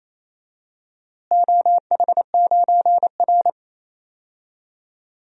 Morse code, heard as a keyed tone, about 28 wpm, 700 Hz. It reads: OH9R
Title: DXCC-100 Call Signs - 4 Characters - Repeated 3x - 28wpm